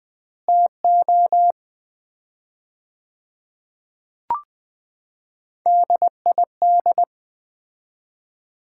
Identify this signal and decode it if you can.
Morse code, heard as a keyed tone, about 20 wpm, 700 Hz. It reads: TO DID
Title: 4x Word-Spacing - Sets of 4 Words - 20wpm